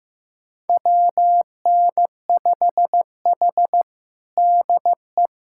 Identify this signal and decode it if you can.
Morse code, heard as a keyed tone, about 15 wpm, 700 Hz. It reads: WN5H DE
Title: USA Callsign DE Callsign - 4 Characters - 15wpm